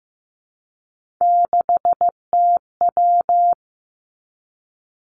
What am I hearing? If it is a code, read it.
Morse code, heard as a keyed tone, about 15 wpm, 700 Hz. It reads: 6TW
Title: Three Challenging Letter-Number Combinations 15wpm